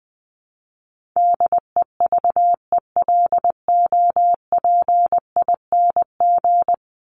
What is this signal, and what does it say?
Morse code, heard as a keyed tone, about 20 wpm, 700 Hz. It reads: DEVELOPING
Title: ING Suffix - Top 100 Words with Base Words 20wpm